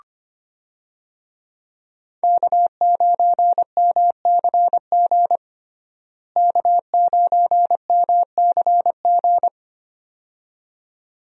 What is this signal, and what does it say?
Morse code, heard as a keyed tone, about 25 wpm, 690 Hz. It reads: K9MCG K9MCG
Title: USA Call Signs - 5 Characters - Repeated 2x - 25wpm